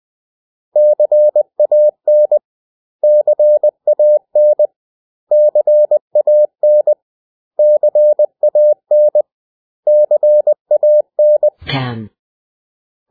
Morse code, heard as a keyed tone, about 20 wpm, 600 Hz. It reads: CAN CAN CAN CAN CAN
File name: cw / 100 Common English Words